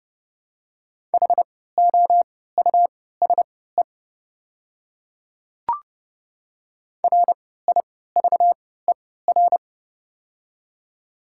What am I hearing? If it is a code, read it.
Morse code, heard as a keyed tone, about 30 wpm, spaced as for 17 wpm, 700 Hz. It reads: HOUSE RIVER
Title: 3x Character Spacing - Top 300 Words - 30wpm